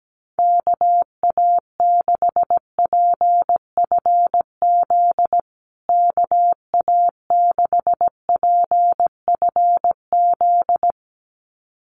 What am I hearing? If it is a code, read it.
Morse code, heard as a keyed tone, about 17 wpm, 700 Hz. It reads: KA6PFZ KA6PFZ
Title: USA Call Signs Repeated 2x 17wpm